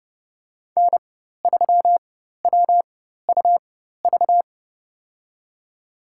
Morse code, heard as a keyed tone, about 30 wpm, spaced as for 14 wpm, 700 Hz. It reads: N3WUV
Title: USA Call Signs - 5 Characters - 4x Character Spacing - 30wpm